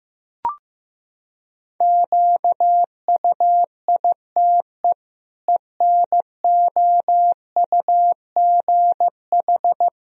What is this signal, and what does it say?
Morse code, heard as a keyed tone, about 15 wpm, 700 Hz. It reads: QUITE ENOUGH
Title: Sentences from Top 1000 Words - Review Part 1 - 15wpm